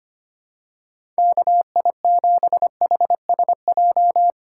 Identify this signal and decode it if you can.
Morse code, heard as a keyed tone, about 25 wpm, 700 Hz. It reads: KI7HSJ
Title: USA Call Signs - 6 Characters - 2x Character Spacing - 25wpm